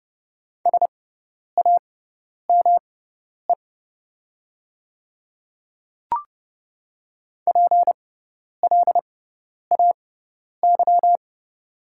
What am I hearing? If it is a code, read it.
Morse code, heard as a keyed tone, about 30 wpm, spaced as for 10 wpm, 700 Hz. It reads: SAME PLAY
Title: 6x Character Spacing - Top 200 Words - 30wpm